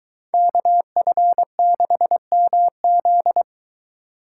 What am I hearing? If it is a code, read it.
Morse code, heard as a keyed tone, about 21 wpm, 700 Hz. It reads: KF6MZ